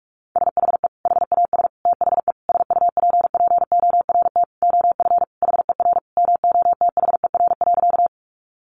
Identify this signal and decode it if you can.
Morse code, heard as a keyed tone, about 44 wpm, 700 Hz. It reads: SHE HAS THE SUPPORT OF HER MOTHER.
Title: Sentences from Top 800 Words 45wpm